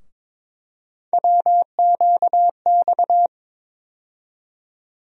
Morse code, heard as a keyed tone, about 22 wpm, 700 Hz. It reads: WQX
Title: USA Call Sign Postfixes - 3 Letters 22wpm